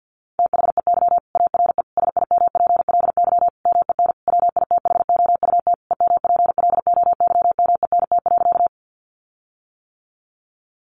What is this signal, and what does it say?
Morse code, heard as a keyed tone, about 50 wpm, 700 Hz. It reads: THEY ARE SIMPLY MEN WITHOUT EMPLOYMENT.
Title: Sentences from Top 1500 Words 50wpm